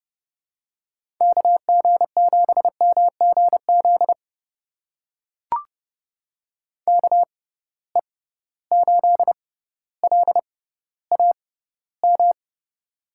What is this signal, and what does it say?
Morse code, heard as a keyed tone, about 30 wpm, spaced as for 15 wpm, 700 Hz. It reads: KG7MGZ KE8LAM
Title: USA Call Signs - 6 Characters - 6x Character Spacing - 30wpm